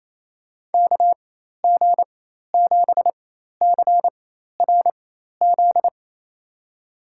Morse code, heard as a keyed tone, about 27 wpm, spaced as for 13 wpm, 700 Hz. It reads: KG7CRZ
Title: USA Call Signs - 6 Characters - 4x Character Spacing - 28wpm